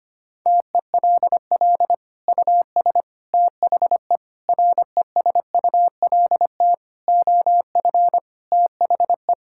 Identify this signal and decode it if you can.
Morse code, heard as a keyed tone, about 25 wpm, 700 Hz. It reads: TELL US THE RESULT OF THE
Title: Sentences from Top 1600 Words 25wpm